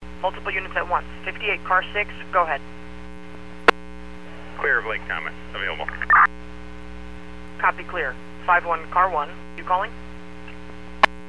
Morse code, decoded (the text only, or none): none